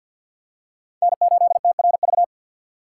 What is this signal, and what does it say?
Morse code, heard as a keyed tone, about 52 wpm, 700 Hz. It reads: N8TRV
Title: USA Call Signs - 5 Characters - 2x Character Spacing - 50wpm